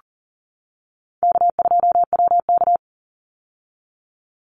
Morse code, heard as a keyed tone, about 40 wpm, 700 Hz. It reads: K2WK
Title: Easy Call Signs - 40wpm